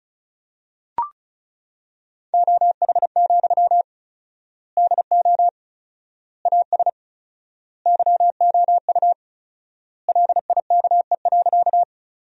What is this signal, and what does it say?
Morse code, heard as a keyed tone, about 36 wpm, 700 Hz. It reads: OH, DO AS YOU LIKE.
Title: 4x Word-Spacing - Sentences from Top 500 Words - 35wpm